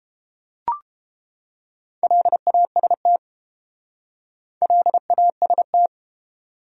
Morse code, heard as a keyed tone, about 30 wpm, 700 Hz. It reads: LAST LAST